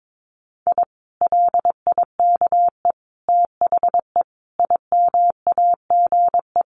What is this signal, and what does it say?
Morse code, heard as a keyed tone, about 22 wpm, 700 Hz. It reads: I LIKE THE IMAGE
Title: Sentences from Top 700 Words 22wpm